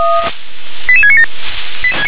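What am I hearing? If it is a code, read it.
Five-tone selective call (ZVEI), heard as quick runs of five tones, tones of about 70 ms. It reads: 88697
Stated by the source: Five-tone ZVEI selective call received by SDR.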